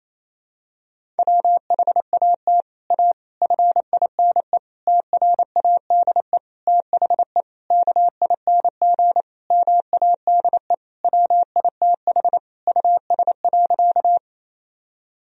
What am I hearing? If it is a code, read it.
Morse code, heard as a keyed tone, about 28 wpm, 700 Hz. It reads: WHAT A FINE TRADE THE KING MADE WITH US.
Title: Sentences from Top 1000 Words - Review Part 4 - 28wpm